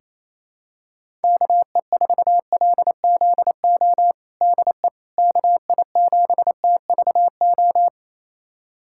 Morse code, heard as a keyed tone, about 28 wpm, 700 Hz. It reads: KE4LZO DE KI7TVO
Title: USA Callsign DE Callsign - 6 Characters - 28wpm